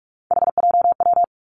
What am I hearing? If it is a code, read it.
Morse code, heard as a keyed tone, about 46 wpm, 700 Hz. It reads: HJW
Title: Three Challenging Letter-Number Combinations 45wpm